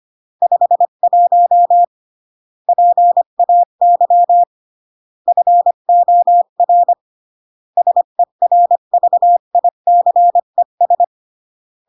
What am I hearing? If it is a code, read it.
Morse code, heard as a keyed tone, about 25 wpm, 700 Hz. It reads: 51 PAY FOR SERVICES
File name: CW / jodorowsky / 82 commandments 25-20